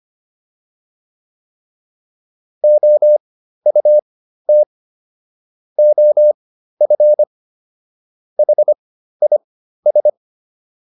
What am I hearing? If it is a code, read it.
Morse code, heard as a keyed tone, about 25 wpm, spaced as for 13 wpm, 600 Hz. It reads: OUT OF HIS